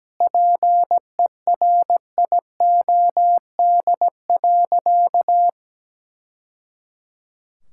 Morse code, heard as a keyed tone, about 17 wpm, 700 Hz. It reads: PERIOD.
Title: Sentences from Top 1000 Words - Review Part 5 - 17wpm